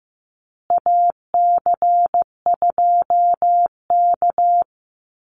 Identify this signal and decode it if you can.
Morse code, heard as a keyed tone, about 15 wpm, 700 Hz. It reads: AC2K
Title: Call Signs Repeated 3x - 15wpm